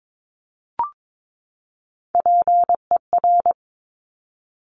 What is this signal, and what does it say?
Morse code, heard as a keyed tone, about 22 wpm, 700 Hz. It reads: PER
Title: Top Three Letter Combinations in Words 22wpm